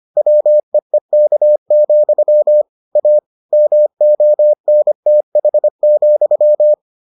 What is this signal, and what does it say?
Morse code, heard as a keyed tone, about 25 wpm, 600 Hz. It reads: WEEK, A MONTH,